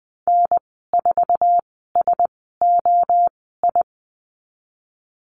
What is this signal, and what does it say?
Morse code, heard as a keyed tone, about 19 wpm, spaced as for 14 wpm, 700 Hz. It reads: N4SOI
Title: 048 Lesson 9g - Number 4 - Call Signs 20wpm